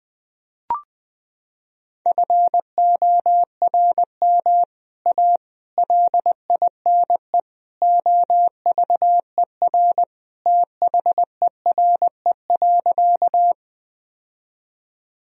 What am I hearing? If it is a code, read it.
Morse code, heard as a keyed tone, about 20 wpm, 700 Hz. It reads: FORM A LINE OVER THERE.
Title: Sentences from Top 1000 Words - Review Part 1 - 20wpm